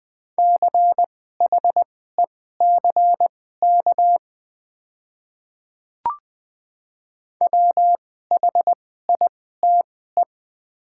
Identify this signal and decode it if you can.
Morse code, heard as a keyed tone, about 20 wpm, spaced as for 15 wpm, 690 Hz. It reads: CHECK WHITE
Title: 2x Character Spacing - Top 500 Words - 20wpm